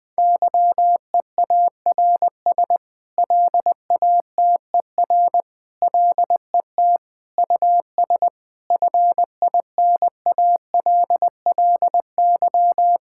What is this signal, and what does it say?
Morse code, heard as a keyed tone, about 20 wpm, 700 Hz. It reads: YEARS LATER LET US FINALLY
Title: Sentences with 8-Word Phrases - 20wpm